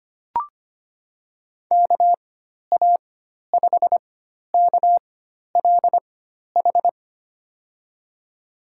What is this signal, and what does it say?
Morse code, heard as a keyed tone, about 25 wpm, spaced as for 12 wpm, 700 Hz. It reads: KA5KLH
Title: USA Call Signs - 4x Character Spacing - 25wpm